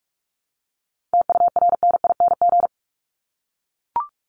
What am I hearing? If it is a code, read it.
Morse code, heard as a keyed tone, about 44 wpm, 700 Hz. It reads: TURNING
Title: ING Suffix - Top 100 Words with Base Words 45wpm